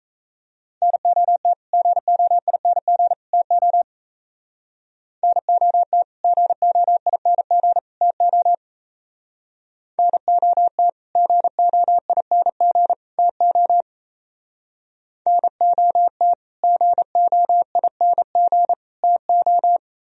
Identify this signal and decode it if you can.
Morse code, heard as a keyed tone, about 36 wpm, 700 Hz. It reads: NOT GOING TO NOT GOING TO NOT GOING TO NOT GOING TO